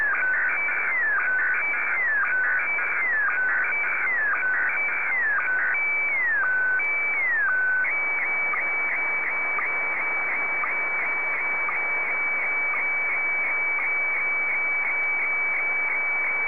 SSTV (slow-scan television), Scottie DX.